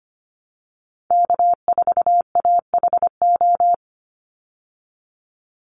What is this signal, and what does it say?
Morse code, heard as a keyed tone, about 25 wpm, 700 Hz. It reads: K4AHO
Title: Call Signs Repeated 3x - 25wpm